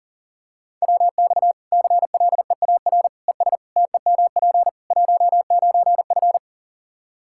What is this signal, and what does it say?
Morse code, heard as a keyed tone, about 40 wpm, 700 Hz. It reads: WX CLEAR ES TEMP 19F
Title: Practice QSO - Exchange 2 - Info - 40wpm